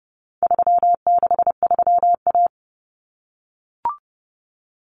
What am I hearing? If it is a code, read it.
Morse code, heard as a keyed tone, about 30 wpm, 700 Hz. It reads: 363A
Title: Common Cut Numbers - 4-Digit 30wpm